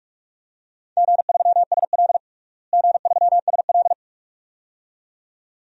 Morse code, heard as a keyed tone, about 46 wpm, 700 Hz. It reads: G3SL G3SL
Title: DXCC-100 Call Signs - 4 Characters - Repeated 2x - 45wpm